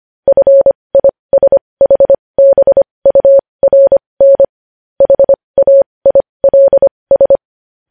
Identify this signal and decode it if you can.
Morse code, heard as a keyed tone, about 25 wpm, spaced as for 20 wpm, 550 Hz. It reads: FISHBURN HAILS
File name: podcast / News headlines-1x-25x20